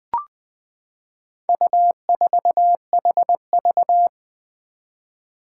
Morse code, heard as a keyed tone, about 20 wpm, 700 Hz. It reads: U4HV
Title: Four Challenging Letter-Number Combinations 20wpm